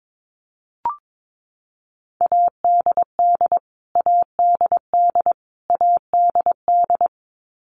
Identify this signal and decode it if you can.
Morse code, heard as a keyed tone, about 22 wpm, 700 Hz. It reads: ADD ADD ADD
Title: Top 500 Words Repeated 22wpm